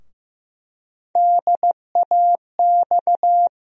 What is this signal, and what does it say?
Morse code, heard as a keyed tone, about 15 wpm, 700 Hz. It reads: DAX
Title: USA Call Sign Postfixes - 3 Letters 15wpm